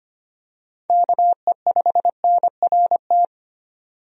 Morse code, heard as a keyed tone, about 25 wpm, 700 Hz. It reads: KE5NRT
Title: USA Call Signs - 6 Characters - 25wpm